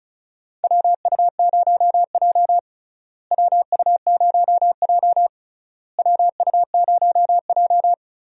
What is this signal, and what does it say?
Morse code, heard as a keyed tone, about 36 wpm, 700 Hz. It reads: WU0J WU0J WU0J
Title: USA Call Signs - 4 Characters - Repeated 3x - 35wpm